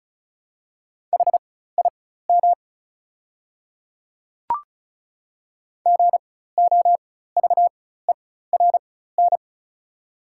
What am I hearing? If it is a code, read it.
Morse code, heard as a keyed tone, about 36 wpm, spaced as for 16 wpm, 690 Hz. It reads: HIM GOVERN
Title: 4x Character Spacing - Top 500 Words - 35wpm